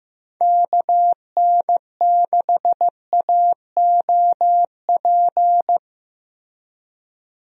Morse code, heard as a keyed tone, about 15 wpm, 700 Hz. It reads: KN6AOP
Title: USA Call Signs - 6 Characters - Repeated 2x - 15wpm